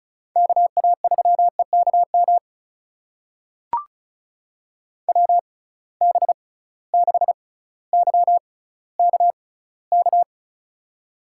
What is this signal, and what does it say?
Morse code, heard as a keyed tone, about 36 wpm, spaced as for 18 wpm, 700 Hz. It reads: KA3EKM WB6YKK
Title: USA Call Signs - 6 Characters - 6x Character Spacing - 35wpm